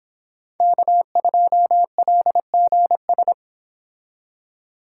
Morse code, heard as a keyed tone, about 26 wpm, 700 Hz. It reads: K2LGS